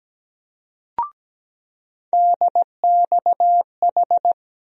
Morse code, heard as a keyed tone, about 17 wpm, 700 Hz. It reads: DXH
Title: USA Call Sign Postfixes - 3 Letters 17wpm